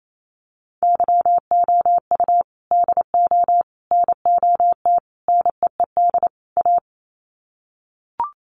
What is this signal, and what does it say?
Morse code, heard as a keyed tone, about 28 wpm, 700 Hz. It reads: YOU DO NOT NEED A
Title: Sets of 5 Words Encore 28wpm